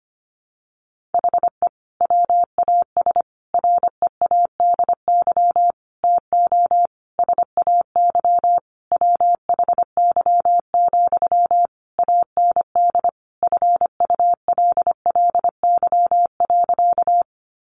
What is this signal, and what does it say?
Morse code, heard as a keyed tone, about 25 wpm, 700 Hz. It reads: HE WAS READY TO SAY WHY, AND FULLY.
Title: Sentences from Top 1300 Words 25wpm